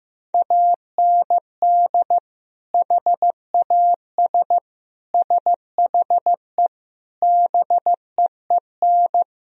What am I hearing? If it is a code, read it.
Morse code, heard as a keyed tone, about 15 wpm, 700 Hz. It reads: AND HAS SHE BEEN
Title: Sentences from Top 1000 Words - Review Part 1 - 15wpm